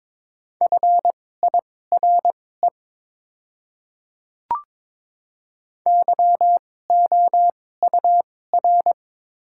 Morse code, heard as a keyed tone, about 22 wpm, spaced as for 16 wpm, 690 Hz. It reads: FIRE YOUR
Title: 2x Character Spacing - Top 500 Words - 22wpm